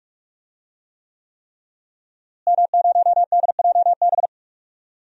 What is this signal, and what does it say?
Morse code, heard as a keyed tone, about 43 wpm, 700 Hz. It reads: M0DJB
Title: DXCC-100 Call Signs - 5 Characters - Repeated 3x - 45wpm